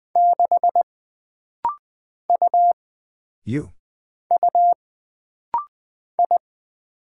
Morse code, heard as a keyed tone, about 20 wpm, 700 Hz. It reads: 6 U U I
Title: Single Character (Letter:Number:Punctuation:Prosign) - Rapid-Fire - 20wpm